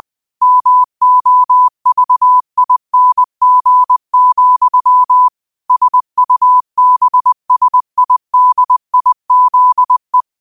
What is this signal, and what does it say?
Morse code, heard as a keyed tone, about 20 wpm, 1000 Hz. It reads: MOVING, SUBSIDIZE